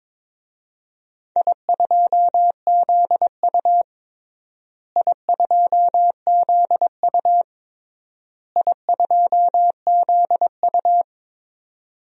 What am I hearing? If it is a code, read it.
Morse code, heard as a keyed tone, about 22 wpm, 700 Hz. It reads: I2ZU I2ZU I2ZU
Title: DXCC-100 Call Signs - 4 Characters - Repeated 3x - 22wpm